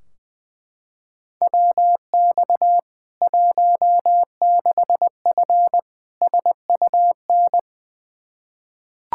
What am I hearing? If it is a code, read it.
Morse code, heard as a keyed tone, about 20 wpm, 700 Hz. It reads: WX 16F SUN
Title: K1USN SST QSO Practice 20wpm